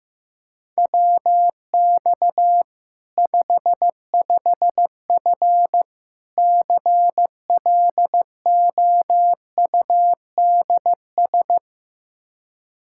Morse code, heard as a keyed tone, about 16 wpm, 700 Hz. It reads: WX 55F CLOUDS
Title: Practice QSO - Exchange 2 - Info - 15wpm